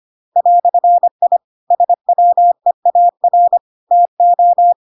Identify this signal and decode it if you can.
Morse code, heard as a keyed tone, about 25 wpm, 700 Hz. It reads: "I SWEAR TO
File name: Pinocchio / Pinocchio ch2 25 25 ik1hge